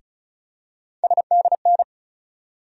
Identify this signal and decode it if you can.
Morse code, heard as a keyed tone, about 34 wpm, 700 Hz. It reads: SDN
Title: USA Call Sign Postfixes - 3 Letters 35wpm